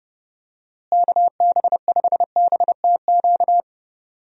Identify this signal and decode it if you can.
Morse code, heard as a keyed tone, about 31 wpm, 700 Hz. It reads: KB5BTQ
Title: USA Call Signs - 6 Characters - Repeated 3x - 30wpm